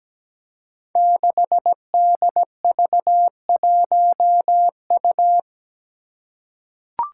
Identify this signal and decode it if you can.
Morse code, heard as a keyed tone, about 17 wpm, 700 Hz. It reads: 6DV1U
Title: Five Challenging Letter-Number Combinations 17wpm